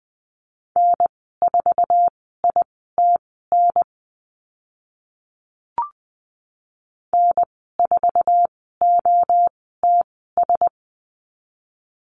Morse code, heard as a keyed tone, about 20 wpm, spaced as for 14 wpm, 700 Hz. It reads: N4ITN N4OTS
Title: 055 Lesson 10g - Letter r - Call Signs 20wpm